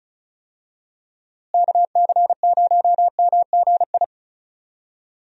Morse code, heard as a keyed tone, about 36 wpm, 700 Hz. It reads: KC0MGI
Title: USA Call Signs - 6 Characters - Repeated 3x - 35wpm